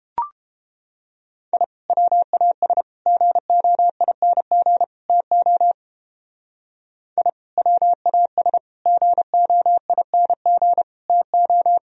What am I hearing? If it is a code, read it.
Morse code, heard as a keyed tone, about 31 wpm, 700 Hz. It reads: I WAS GOING TO I WAS GOING TO